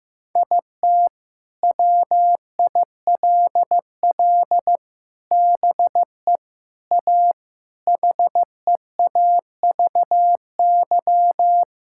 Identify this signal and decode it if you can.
Morse code, heard as a keyed tone, about 15 wpm, 700 Hz. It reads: IT WILL BE A HEAVY
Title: Sentences from Top 1600 Words 15wpm